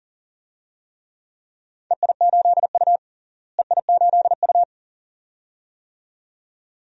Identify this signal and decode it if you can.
Morse code, heard as a keyed tone, about 40 wpm, 700 Hz. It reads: EI8U EI8U
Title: DXCC-100 Call Signs - 4 Characters - Repeated 2x - 40wpm